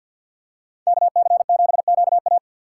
Morse code, heard as a keyed tone, about 50 wpm, 700 Hz. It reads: KC7QA
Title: USA Call Signs - 5 Characters - 3x Character Spacing - 50wpm